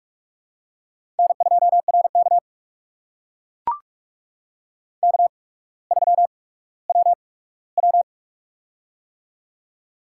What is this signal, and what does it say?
Morse code, heard as a keyed tone, about 44 wpm, spaced as for 19 wpm, 700 Hz. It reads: N2RK K3WW
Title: USA Call Signs - 4 Characters - 8x Character Spacing - 45wpm